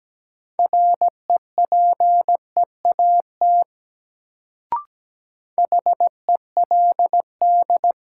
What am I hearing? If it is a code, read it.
Morse code, heard as a keyed tone, about 17 wpm, 700 Hz. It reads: REPEAT HELD
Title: Sentences from Top 1600 Words - New Word Drill - 17wpm